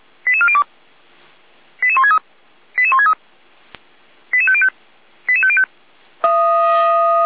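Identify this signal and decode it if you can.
Five-tone selective call (ZVEI), heard as quick runs of five tones, tones of about 70 ms. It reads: 88402 88163 88163 88505 88505
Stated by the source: SDR-received ZVEI five-tone selective call, code 88352.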